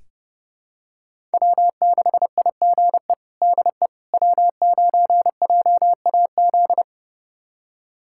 Morse code, heard as a keyed tone, about 30 wpm, 700 Hz. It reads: W6IGE DE W9JAZ
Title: USA Callsign DE Callsign - 5 Characters - 30wpm